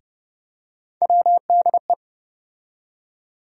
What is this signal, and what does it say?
Morse code, heard as a keyed tone, about 31 wpm, 700 Hz. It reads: WDE
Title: USA Call Sign Postfixes - 3 Letters 30wpm